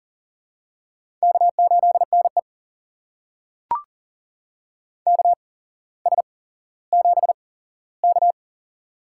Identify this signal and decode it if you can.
Morse code, heard as a keyed tone, about 40 wpm, spaced as for 17 wpm, 700 Hz. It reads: K8NE KS7K
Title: USA Call Signs - 4 Characters - 8x Character Spacing - 40wpm